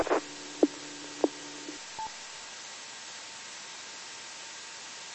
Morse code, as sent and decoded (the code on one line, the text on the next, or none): none